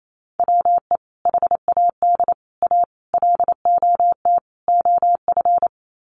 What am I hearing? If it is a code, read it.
Morse code, heard as a keyed tone, about 28 wpm, 690 Hz. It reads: WE HAD A LOT OF
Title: Sets of 5 Words Encore 28wpm